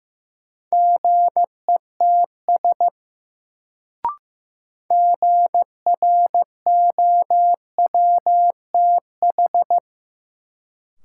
Morse code, heard as a keyed tone, about 15 wpm, 700 Hz. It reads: GETS GROWTH
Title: Sentences from Top 1300 Words - New Word Drill Alt - 15wpm